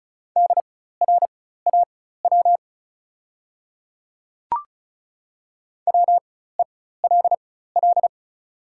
Morse code, heard as a keyed tone, about 34 wpm, spaced as for 16 wpm, 690 Hz. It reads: DRAW WELL
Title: 4x Character Spacing - Top 500 Words - 35wpm